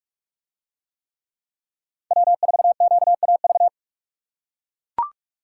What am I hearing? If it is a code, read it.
Morse code, heard as a keyed tone, about 44 wpm, 700 Hz. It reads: W4QAV